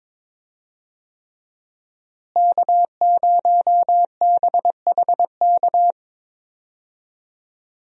Morse code, heard as a keyed tone, about 22 wpm, 700 Hz. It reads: K0BHK
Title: USA Call Signs - 5 Characters - Repeated 3x - 22wpm